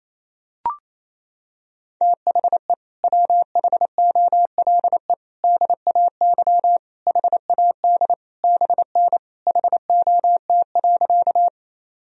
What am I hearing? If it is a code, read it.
Morse code, heard as a keyed tone, about 28 wpm, 700 Hz. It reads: THE WHOLE DAY HAD BN HOT.
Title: Sentences from Top 500 Words and Abbreviations 28wpm